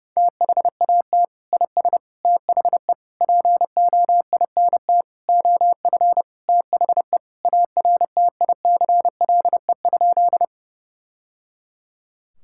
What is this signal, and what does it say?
Morse code, heard as a keyed tone, about 30 wpm, 700 Hz. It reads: THAT IS THE POINT OF THE ARTICLE?
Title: Sentences from Top 1000 Words - Review Part 5 - 30wpm